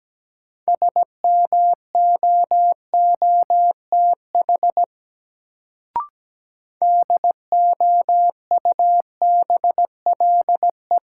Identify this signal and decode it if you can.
Morse code, heard as a keyed tone, about 17 wpm, 700 Hz. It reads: SMOOTH DOUBLE
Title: Sentences from Top 1400 Words - New Word Drill - 17wpm